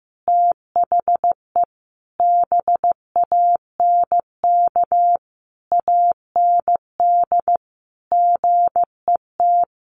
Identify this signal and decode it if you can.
Morse code, heard as a keyed tone, about 15 wpm, 700 Hz. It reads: THE BANK AND GET
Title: Sentences from Top 1500 Words 15wpm